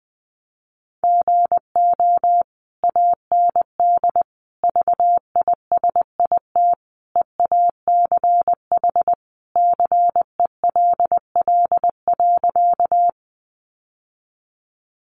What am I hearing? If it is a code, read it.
Morse code, heard as a keyed tone, about 20 wpm, 700 Hz. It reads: GO AND VISIT EACH CELL.
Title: Sentences from Top 1100 Words 20wpm